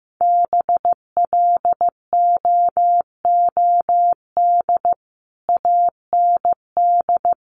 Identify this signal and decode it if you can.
Morse code, heard as a keyed tone, about 15 wpm, 700 Hz. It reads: BLOOD AND
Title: Sentences from Top 1100 Words 15wpm